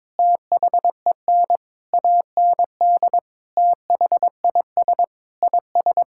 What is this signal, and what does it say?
Morse code, heard as a keyed tone, about 21 wpm, 700 Hz. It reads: THEN AND THIS IS
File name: practice / 22wpm / Sentences with 8-Word Phrases - 22wpm